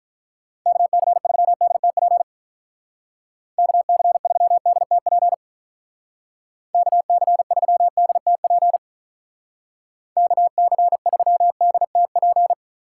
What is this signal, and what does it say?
Morse code, heard as a keyed tone, about 43 wpm, 700 Hz. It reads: KC3DTP KC3DTP KC3DTP KC3DTP
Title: Speed Racing - USA Call Signs 35wpm